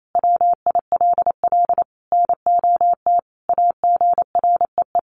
Morse code, heard as a keyed tone, about 28 wpm, 700 Hz. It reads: WILL NOT AGREE
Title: Sentences from Top 1100 Words 28wpm